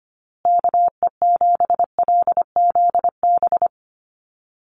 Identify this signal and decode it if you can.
Morse code, heard as a keyed tone, about 25 wpm, 700 Hz. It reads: KE7LZB
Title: USA Call Signs Repeated 2x 25wpm